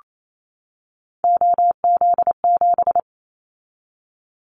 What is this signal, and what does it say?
Morse code, heard as a keyed tone, about 28 wpm, 700 Hz. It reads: OZ7
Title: Callsign Prefixes for Top 100 DXCC - 28wpm